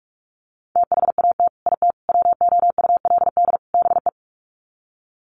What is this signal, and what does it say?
Morse code, heard as a keyed tone, about 44 wpm, 700 Hz. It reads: THAT IT WOULD BE
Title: Sets of 4 Words 45wpm